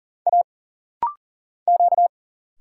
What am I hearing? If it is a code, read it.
Morse code, heard as a keyed tone, about 40 wpm, 700 Hz. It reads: A Q
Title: Single Letter-Number Alt - Rapid-Fire 40wpm